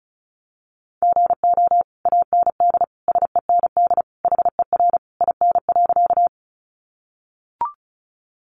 Morse code, heard as a keyed tone, about 34 wpm, 700 Hz. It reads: GO AND SEND HER IN.
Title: Sentences from Top 1500 Words 35wpm